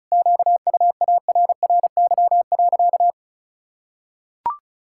Morse code, heard as a keyed tone, about 36 wpm, 700 Hz. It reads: QUARRY.
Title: Pangram Challenge v2 - 35wpm